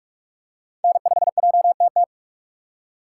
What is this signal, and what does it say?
Morse code, heard as a keyed tone, about 43 wpm, 700 Hz. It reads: N5JTT